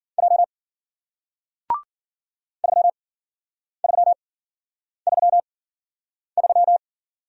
Morse code, heard as a keyed tone, about 54 wpm, spaced as for 9 wpm, 700 Hz. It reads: J 3333